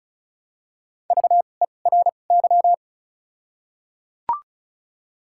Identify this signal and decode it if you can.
Morse code, heard as a keyed tone, about 36 wpm, 700 Hz. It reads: V E R Y
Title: 2x Character Spacing - Top 200 Words - 35wpm